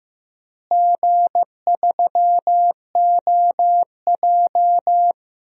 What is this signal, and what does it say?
Morse code, heard as a keyed tone, about 15 wpm, 700 Hz. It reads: G3OJ
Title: DXCC-100 Call Signs - 4 Characters - Repeated 2x - 15wpm